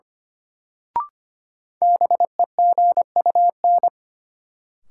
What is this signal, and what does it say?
Morse code, heard as a keyed tone, about 25 wpm, 700 Hz. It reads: BEGUN
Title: Sentences from Top 1000 Words - Review Part 6 - New Word Drill - 25wpm